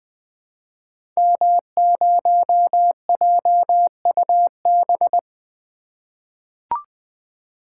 Morse code, heard as a keyed tone, about 20 wpm, 700 Hz. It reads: M0JUB
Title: Code Groups - 5 Letter:Numbers - 20wpm